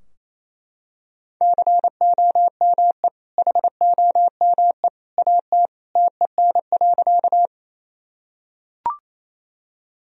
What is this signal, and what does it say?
Morse code, heard as a keyed tone, about 28 wpm, 700 Hz. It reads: COME HOME AT TEN.
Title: Sentences from Top 1000 Words - Review Part 2 - 28wpm